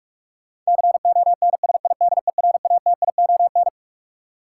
Morse code, heard as a keyed tone, about 46 wpm, 700 Hz. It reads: CONSIDERATION
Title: Sentences from Top 1500 Words - New Word Drill Alt - 45wpm